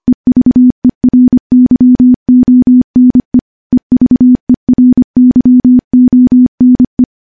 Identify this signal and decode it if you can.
Morse code, heard as a keyed tone, about 25 wpm, 250 Hz. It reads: EVERYONE EVERYONE
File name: morse / morse-training-25WPM